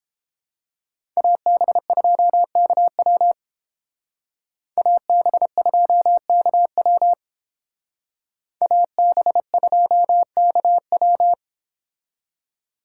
Morse code, heard as a keyed tone, about 30 wpm, 700 Hz. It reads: AB2KW AB2KW AB2KW